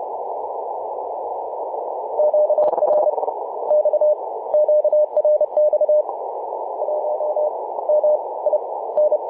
Morse code, heard as a keyed tone, about 30 wpm, spaced as for 20 wpm, 600 Hz, with noise. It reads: QRX QRX 1 MIN
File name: VU4A 21.012MHz CW